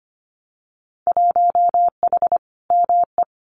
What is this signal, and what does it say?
Morse code, heard as a keyed tone, about 25 wpm, 700 Hz. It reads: 1H ME
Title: Winter Field Day Exchanges 25wpm